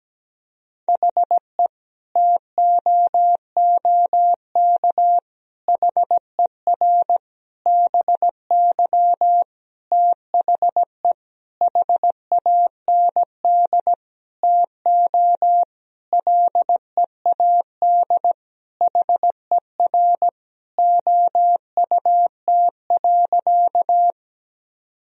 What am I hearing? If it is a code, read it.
Morse code, heard as a keyed tone, about 17 wpm, 700 Hz. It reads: HE TOOK HER BY THE HAND TO LEAD HER OUT.
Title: Sentences from Top 1000 Words - Review Part 2 - 17wpm